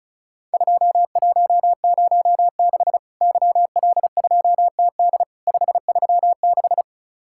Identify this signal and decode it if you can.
Morse code, heard as a keyed tone, about 36 wpm, 700 Hz. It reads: 2106 YL2TD 536
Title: Worked All Europe DX Contest - QTC Exchange - 35wpm